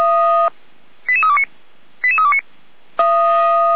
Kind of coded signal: five-tone selective call